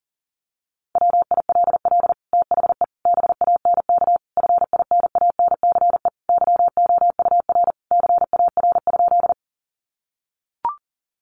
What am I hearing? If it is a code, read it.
Morse code, heard as a keyed tone, about 40 wpm, 700 Hz. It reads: WILL THE BANK FINANCE YOUR CAR?
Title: Sentences from Top 1500 Words 40wpm